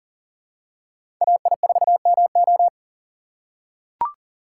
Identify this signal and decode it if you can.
Morse code, heard as a keyed tone, about 40 wpm, 700 Hz. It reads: AI4MO
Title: USA Call Signs - 5 Characters - 6x Character Spacing - 40wpm